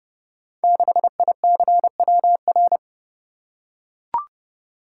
Morse code, heard as a keyed tone, about 30 wpm, 700 Hz. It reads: 6ICWR
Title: Code Groups - 5 Letter:Numbers - 30wpm